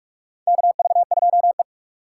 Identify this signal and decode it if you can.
Morse code, heard as a keyed tone, about 45 wpm, 700 Hz. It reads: KV2E
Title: USA Call Signs - 4 Characters - 2x Character Spacing - 45wpm